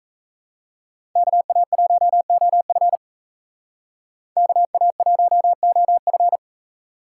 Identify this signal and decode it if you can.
Morse code, heard as a keyed tone, about 32 wpm, 700 Hz. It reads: KA1OF KA1OF
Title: Speed Racing - USA 2x2 Call Signs 28wpm